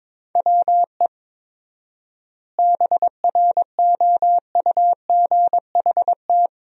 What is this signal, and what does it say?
Morse code, heard as a keyed tone, about 22 wpm, 700 Hz. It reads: WE BROUGHT
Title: 4x Word-Spacing - Sentences from Top 500 Words - 22wpm